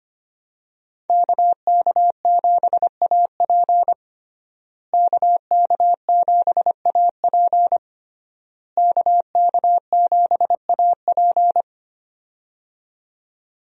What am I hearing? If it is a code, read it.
Morse code, heard as a keyed tone, about 25 wpm, 700 Hz. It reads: KK7AP KK7AP KK7AP
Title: USA Call Signs - 5 Characters - Repeated 3x - 25wpm